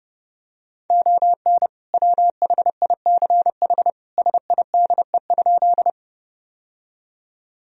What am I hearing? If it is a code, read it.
Morse code, heard as a keyed tone, about 32 wpm, 700 Hz. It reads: ON WHICH SIDE?
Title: Sentences from Top 1000 Words - Review Part 1 - 30wpm